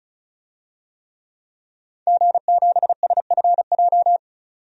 Morse code, heard as a keyed tone, about 36 wpm, 700 Hz. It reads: G7SFJ
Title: DXCC-100 Call Signs - 5 Characters - 35wpm